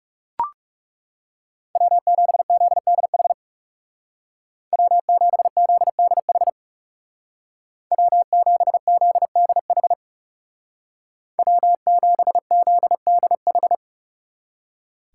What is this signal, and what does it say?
Morse code, heard as a keyed tone, about 37 wpm, 700 Hz. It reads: W7ZDH W7ZDH W7ZDH W7ZDH
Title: Speed Racing - USA Call Signs 30wpm